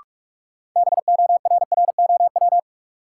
Morse code, heard as a keyed tone, about 46 wpm, 700 Hz. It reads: BORROW
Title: Sentences from Top 1500 Words - New Word Drill Alt - 45wpm